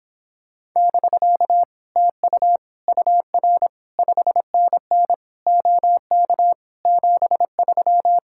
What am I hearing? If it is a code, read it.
Morse code, heard as a keyed tone, about 26 wpm, 700 Hz. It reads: <BK> TU UR 5NN OK 73